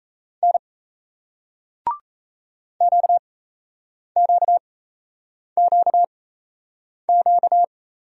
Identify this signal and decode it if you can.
Morse code, heard as a keyed tone, about 35 wpm, spaced as for 8 wpm, 700 Hz. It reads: N QQQQ